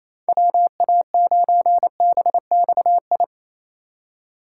Morse code, heard as a keyed tone, about 28 wpm, 700 Hz. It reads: WA9BXI
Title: USA Call Signs - 6 Characters - 4x Character Spacing - 28wpm